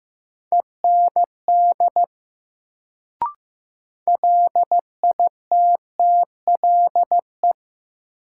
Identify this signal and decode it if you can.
Morse code, heard as a keyed tone, about 15 wpm, 700 Hz. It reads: END LITTLE
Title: Sentences from Top 200 Words - New Word Drill - 15wpm